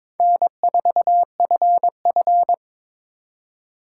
Morse code, heard as a keyed tone, about 26 wpm, 700 Hz. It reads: N4FF
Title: Speed Racing - USA 1x2 Call Signs 22wpm